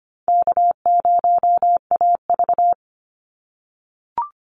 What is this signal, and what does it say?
Morse code, heard as a keyed tone, about 25 wpm, 700 Hz. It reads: K0AV
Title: Easy Call Signs - 25wpm